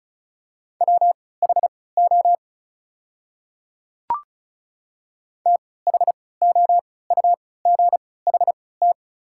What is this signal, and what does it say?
Morse code, heard as a keyed tone, about 36 wpm, spaced as for 20 wpm, 690 Hz. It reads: WHO THOUGHT
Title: 3x Character Spacing - Top 500 Words - 35wpm